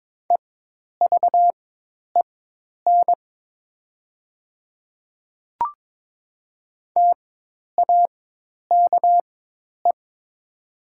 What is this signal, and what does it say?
Morse code, heard as a keyed tone, about 22 wpm, spaced as for 10 wpm, 700 Hz. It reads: EVEN TAKE
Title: 4x Character Spacing - Top 200 Words - 22wpm